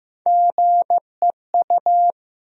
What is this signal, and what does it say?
Morse code, heard as a keyed tone, about 15 wpm, 700 Hz. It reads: GEU
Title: USA Call Sign Postfixes - 3 Letters 15wpm